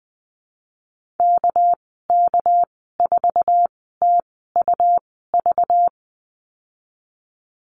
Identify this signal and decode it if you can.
Morse code, heard as a keyed tone, about 20 wpm, spaced as for 15 wpm, 700 Hz. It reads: KK4TUV
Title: 219 Lesson 35g - Number 0 - Call Signs 20wpm